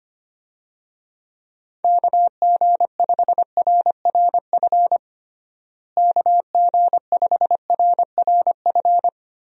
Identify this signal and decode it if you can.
Morse code, heard as a keyed tone, about 25 wpm, 700 Hz. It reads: KG5RRF KG5RRF
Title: USA Call Signs - 6 Characters - Repeated 2x - 25wpm